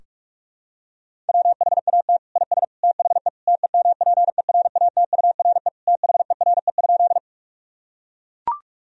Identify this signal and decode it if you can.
Morse code, heard as a keyed tone, about 41 wpm, 700 Hz. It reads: WHAT IS THE TEMPERATURE THERE?
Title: Sentences from Top 1000 Words - Review Part 4 - 45wpm